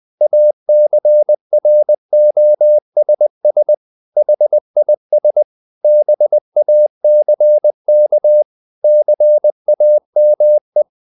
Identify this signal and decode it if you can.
Morse code, heard as a keyed tone, about 20 wpm, 600 Hz. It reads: ACROSS HIS BACK CAME
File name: the old man and the sea / the old man and the sea part 1 20wpm